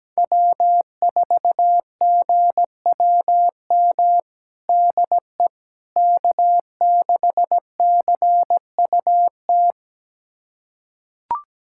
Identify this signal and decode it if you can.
Morse code, heard as a keyed tone, about 17 wpm, 700 Hz. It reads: W4GWM DE K6CUT
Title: USA Callsign DE Callsign - 5 Characters - 17wpm